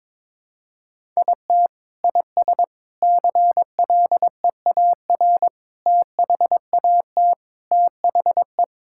Morse code, heard as a keyed tone, about 22 wpm, 700 Hz. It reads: IT IS CLEAR THAT THE
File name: practice / 22wpm / Sentences with 4-Word Phrases - 22wpm